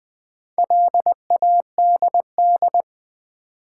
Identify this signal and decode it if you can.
Morse code, heard as a keyed tone, about 20 wpm, 700 Hz. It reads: LADD
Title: Build A Word - 2 Letters At A Time - 6-Letter Words - 20wpm